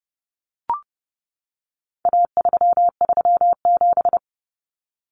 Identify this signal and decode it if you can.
Morse code, heard as a keyed tone, about 30 wpm, 700 Hz. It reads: A337
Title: Common Cut Numbers - 4-Digit 30wpm